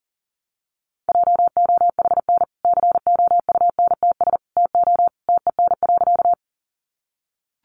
Morse code, heard as a keyed tone, about 40 wpm, 700 Hz. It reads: JOHN COUNTS TO TEN.
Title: Easy Phrases 40wpm